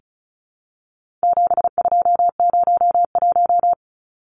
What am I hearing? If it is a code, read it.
Morse code, heard as a keyed tone, about 34 wpm, 700 Hz. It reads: 7201
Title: Numbers - 4-Digit 35wpm